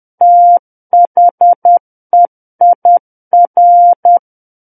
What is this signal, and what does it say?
Morse code, heard as a keyed tone, about 10 wpm, 700 Hz. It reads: THEIR
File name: Book One / 01-Chapter 1 - 10 WPM